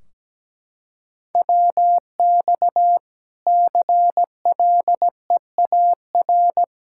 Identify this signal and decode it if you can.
Morse code, heard as a keyed tone, about 17 wpm, 700 Hz. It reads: WX CLEAR
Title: K1USN SST QSO Practice 17wpm